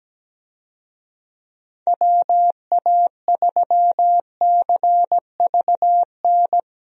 Morse code, heard as a keyed tone, about 17 wpm, 700 Hz. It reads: WA3CVN
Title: USA Call Signs - 6 Characters - Repeated 3x - 17wpm